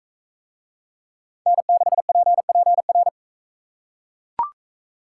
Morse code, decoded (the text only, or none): N6PPR